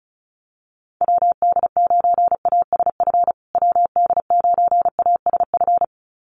Morse code, heard as a keyed tone, about 34 wpm, 700 Hz. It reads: WD9ASF WD9ASF
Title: USA Call Signs Repeated 2x 35wpm